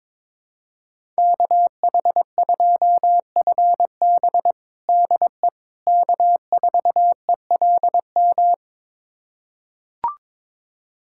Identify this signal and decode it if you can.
Morse code, heard as a keyed tone, about 22 wpm, 700 Hz. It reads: KH2FB DE K4ELM
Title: USA Callsign DE Callsign - 5 Characters - 22wpm